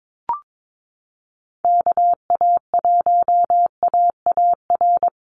Morse code, heard as a keyed tone, about 22 wpm, 700 Hz. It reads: KA1AAR
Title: USA Call Signs 22wpm